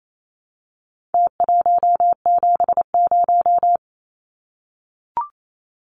Morse code, heard as a keyed tone, about 28 wpm, 700 Hz. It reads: T170
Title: Numbers - Contest Style - 4-Digit 28wpm